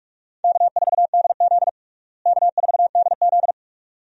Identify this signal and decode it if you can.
Morse code, heard as a keyed tone, about 46 wpm, 700 Hz. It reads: K4DZ K4DZ
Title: USA Call Signs - 4 Characters - Repeated 2x - 45wpm